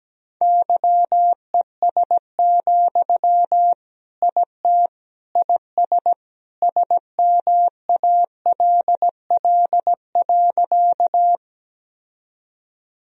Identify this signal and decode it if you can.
Morse code, heard as a keyed tone, about 17 wpm, 700 Hz. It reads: YES, IT IS SMALL.
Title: Sentences from Top 1000 Words - Review Part 1 - 17wpm